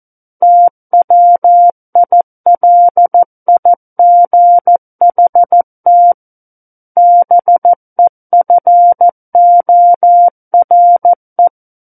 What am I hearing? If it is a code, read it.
Morse code, heard as a keyed tone, about 14 wpm, 700 Hz. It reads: TWILIGHT BEFORE
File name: Book One / 05-Chapter 5 - 14 WPM